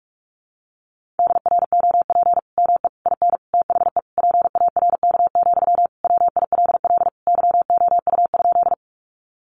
Morse code, heard as a keyed tone, about 44 wpm, 700 Hz. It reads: DROP ME IN THE PARK, WILL YOU?
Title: Sentences from Top 1400 Words 45wpm